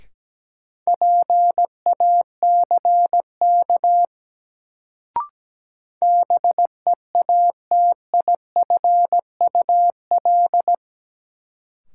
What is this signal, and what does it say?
Morse code, heard as a keyed tone, about 17 wpm, 700 Hz. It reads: PACK BEATIFUL
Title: Sentences from Top 1300 Words - New Word Drill - 17wpm